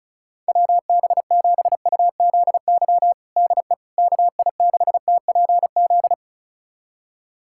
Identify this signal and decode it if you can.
Morse code, heard as a keyed tone, about 36 wpm, 700 Hz. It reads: WB7UZY DE KI6TPZ
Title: USA Callsign DE Callsign - 6 Characters - 35wpm